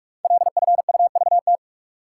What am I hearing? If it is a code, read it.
Morse code, heard as a keyed tone, about 44 wpm, 700 Hz. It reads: LFUVT
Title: Code Groups - 5 Letters - 45wpm